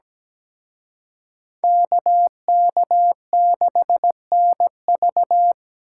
Morse code, heard as a keyed tone, about 20 wpm, 700 Hz. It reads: KK6NV
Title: Speed Racing - USA 2x2 Call Signs 17wpm